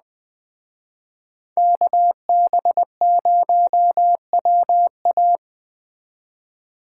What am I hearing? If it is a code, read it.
Morse code, heard as a keyed tone, about 22 wpm, 700 Hz. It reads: KB0WA